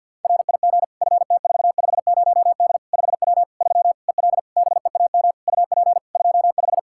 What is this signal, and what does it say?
Morse code, heard as a keyed tone, about 48 wpm, 700 Hz. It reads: RIG FT450D 5W 3 EL BEAM UP 25
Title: Practice QSO - Exchange 2 - Info - 50wpm